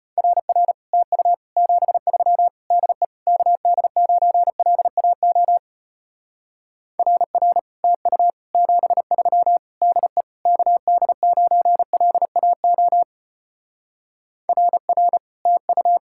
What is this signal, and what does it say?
Morse code, heard as a keyed tone, about 33 wpm, 700 Hz. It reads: RR TU 73 DE KD9LAO RR TU 73 DE KD9LAO RR TU